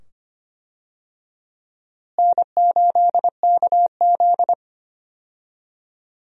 Morse code, heard as a keyed tone, about 25 wpm, 700 Hz. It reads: N8KZ
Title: USA Call Signs - 4 Characters - Repeated 2x - 25wpm